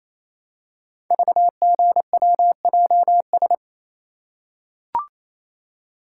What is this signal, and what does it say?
Morse code, heard as a keyed tone, about 28 wpm, 700 Hz. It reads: VGWJS
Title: Code Groups - 5 Letters - 28wpm